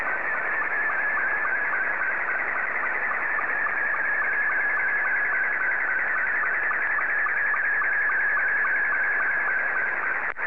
SSTV (slow-scan television), Scottie 2.
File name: sstv / S2-VK8VTX